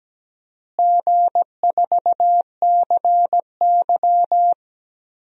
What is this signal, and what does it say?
Morse code, heard as a keyed tone, about 17 wpm, 700 Hz. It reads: G4CY
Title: DXCC-100 Call Signs - 4 Characters - Repeated 2x - 17wpm